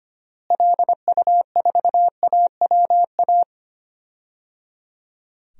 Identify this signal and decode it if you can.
Morse code, heard as a keyed tone, about 25 wpm, 700 Hz. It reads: LU4AWA
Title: DXCC-100 Call Signs - 6 Characters - Repeated 3x - 25wpm